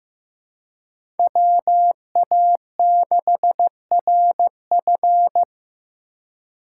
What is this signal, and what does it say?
Morse code, heard as a keyed tone, about 15 wpm, 700 Hz. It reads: WA6RF
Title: USA Call Signs - 5 Characters - 2x Character Spacing - 15wpm